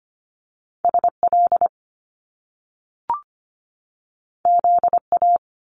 Morse code, heard as a keyed tone, about 25 wpm, 700 Hz. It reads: S L Z A
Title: Top Two Letter Combinations in Words 25wpm